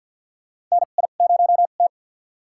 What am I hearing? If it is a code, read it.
Morse code, heard as a keyed tone, about 50 wpm, spaced as for 36 wpm, 700 Hz. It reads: NI0T
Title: USA Call Signs - 4 Characters - 2x Character Spacing - 50wpm